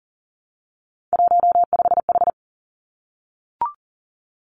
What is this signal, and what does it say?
Morse code, heard as a keyed tone, about 38 wpm, 700 Hz. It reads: 15H
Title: Three Challenging Letter-Number Combinations 40wpm